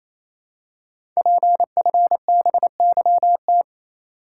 Morse code, heard as a keyed tone, about 29 wpm, 700 Hz. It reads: PFBYT
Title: Code Groups - 5 Letters - 28wpm